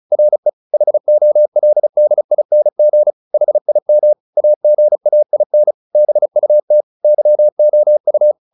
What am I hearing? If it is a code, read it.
Morse code, heard as a keyed tone, about 34 wpm, 600 Hz. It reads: RE HOLDING HIM AGAIN BUT YOU
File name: the old man and the sea 2 35wpm